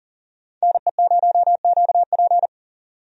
Morse code, heard as a keyed tone, about 40 wpm, 700 Hz. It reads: NE0QP
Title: USA Call Signs - 5 Characters - 40wpm